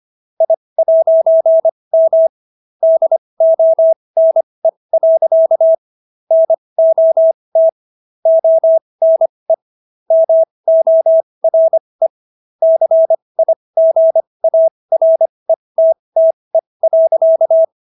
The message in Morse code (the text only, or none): I'M DONE. NOT ONE MORE CIGARETTE.